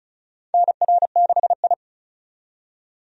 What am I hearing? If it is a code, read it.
Morse code, heard as a keyed tone, about 36 wpm, 700 Hz. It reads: NR6I
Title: USA Call Signs - 4 Characters - 35wpm